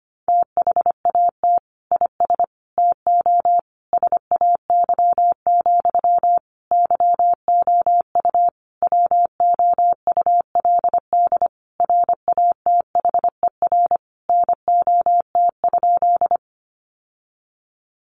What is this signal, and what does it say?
Morse code, heard as a keyed tone, about 25 wpm, 700 Hz. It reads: THAT IS TO SAY, YOU WOULD RATHER NOT?
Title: Sentences from Top 700 Words 25wpm